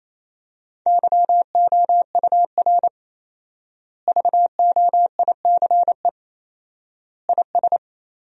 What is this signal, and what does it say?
Morse code, heard as a keyed tone, about 28 wpm, 700 Hz. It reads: YOUR VOICE IS
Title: 4x Word-Spacing - Sentences from Top 500 Words - 28wpm